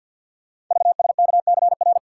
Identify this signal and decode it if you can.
Morse code, heard as a keyed tone, about 50 wpm, 700 Hz. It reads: VSKCR
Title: Code Groups - 5 Letters - 50wpm